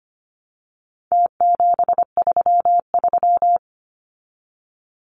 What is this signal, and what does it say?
Morse code, heard as a keyed tone, about 25 wpm, 700 Hz. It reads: T733
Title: Numbers - Contest Style - 4-Digit 25wpm